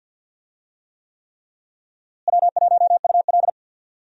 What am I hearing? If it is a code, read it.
Morse code, heard as a keyed tone, about 47 wpm, 700 Hz. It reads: W1UL
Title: USA Call Signs - 4 Characters - Repeated 3x - 50wpm